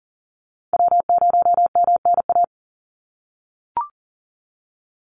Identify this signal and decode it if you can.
Morse code, heard as a keyed tone, about 40 wpm, 700 Hz. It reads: W0MNA
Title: NA SOTA Chaser Callsigns 40wpm